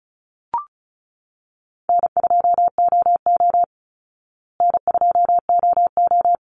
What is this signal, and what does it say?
Morse code, heard as a keyed tone, about 34 wpm, 700 Hz. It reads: N2OO N2OO
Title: Easy Call Signs Repeated 3x - 35wpm